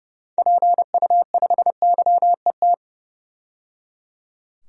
Morse code, heard as a keyed tone, about 30 wpm, 700 Hz. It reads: PU5YET
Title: DXCC-100 Call Signs - 6 Characters - 30wpm